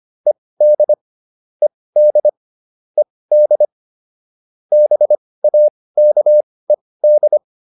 Morse code, heard as a keyed tone, about 25 wpm, spaced as for 18 wpm, 600 Hz. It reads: ED ED ED BAKED